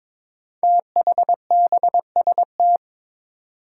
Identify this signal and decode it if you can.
Morse code, heard as a keyed tone, about 22 wpm, 700 Hz. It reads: THBST
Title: Code Groups - 5 Letters - 22wpm